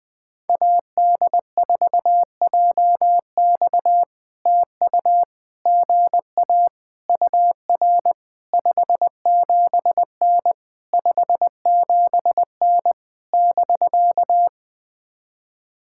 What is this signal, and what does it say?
Morse code, heard as a keyed tone, about 20 wpm, 700 Hz. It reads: AD4JX TU GA UR 57N 57N <BK>
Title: POTA QSO Practice - 20wpm